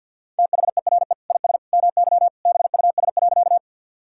Morse code, heard as a keyed tone, about 50 wpm, 700 Hz. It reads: THERE IS MY BUS.
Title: Sentences from Top 1000 Words - Review Part 6 - 50wpm